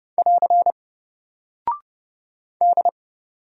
Morse code, heard as a keyed tone, about 29 wpm, 700 Hz. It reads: <AR> D
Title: Single Character (Letter:Number:Punctuation:Prosign) Alt 30wpm